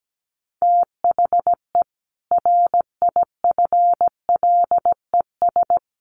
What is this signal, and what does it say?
Morse code, heard as a keyed tone, about 17 wpm, 700 Hz. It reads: THE RIFLES
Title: A Sound Of Thunder 17wpm